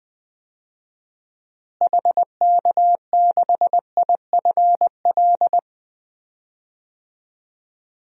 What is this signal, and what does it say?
Morse code, heard as a keyed tone, about 20 wpm, 700 Hz. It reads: HK6IFL
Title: DXCC-100 Call Signs - 6 Characters - Repeated 2x - 20wpm